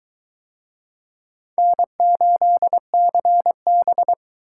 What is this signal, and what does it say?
Morse code, heard as a keyed tone, about 20 wpm, 700 Hz. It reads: N8CB